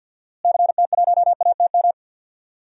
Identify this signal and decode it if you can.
Morse code, heard as a keyed tone, about 46 wpm, 700 Hz. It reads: CT1ATM
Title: DXCC-100 Call Signs - 6 Characters - Repeated 2x - 50wpm